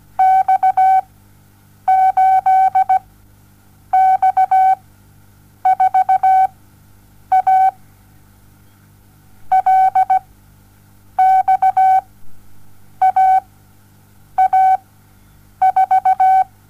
Morse code, heard as a keyed tone, about 17 wpm, spaced as for 7 wpm, 760 Hz. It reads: X8X4A LXAA4
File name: Onlinelehrgang / morsen / DJ4UF Morsen07